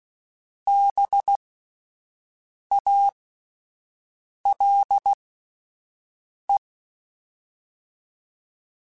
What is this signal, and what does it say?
Morse code, heard as a keyed tone, about 16 wpm, spaced as for 5 wpm, 800 Hz. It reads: BALE
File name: Spaced Alphas / Spaced Alphas mp3s / Bs sp @10